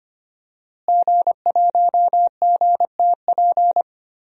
Morse code, heard as a keyed tone, about 25 wpm, 700 Hz. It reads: G1GTP
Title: DXCC-100 Call Signs - 5 Characters - Repeated 3x - 25wpm